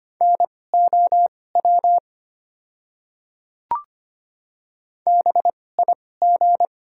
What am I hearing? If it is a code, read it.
Morse code, heard as a keyed tone, about 25 wpm, spaced as for 18 wpm, 700 Hz. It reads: NOW BIG
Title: 2x Character Spacing - Top 300 Words - 25wpm